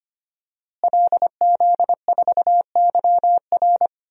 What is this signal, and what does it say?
Morse code, heard as a keyed tone, about 25 wpm, 700 Hz. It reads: LZ4YR
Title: DXCC-100 Call Signs - 5 Characters - Repeated 3x - 25wpm